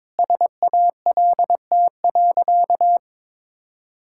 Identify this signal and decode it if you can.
Morse code, heard as a keyed tone, about 22 wpm, 700 Hz. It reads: SALT.
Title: Sentences from Top 1000 Words - Review Part 5 - 22wpm